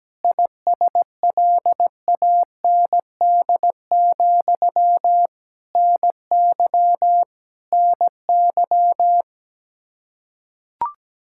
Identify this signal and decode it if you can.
Morse code, heard as a keyed tone, about 17 wpm, 700 Hz. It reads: ISLAND, NY NY
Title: Practice QSO - Exchange 1 - Beginning - 17wpm